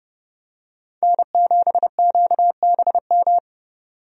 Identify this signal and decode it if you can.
Morse code, heard as a keyed tone, about 30 wpm, 700 Hz. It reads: N7QBM
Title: USA Call Signs - 5 Characters - 30wpm